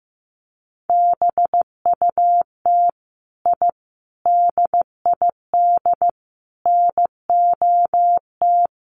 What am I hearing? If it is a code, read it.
Morse code, heard as a keyed tone, about 15 wpm, 700 Hz. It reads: BUT I DID NOT
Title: Sets of 5 Words Encore 15wpm